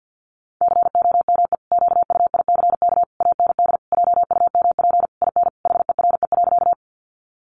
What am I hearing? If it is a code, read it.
Morse code, heard as a keyed tone, about 50 wpm, 700 Hz. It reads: COME QUICK AND JUMP IN HERE.
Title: Sentences from Top 1300 Words 50wpm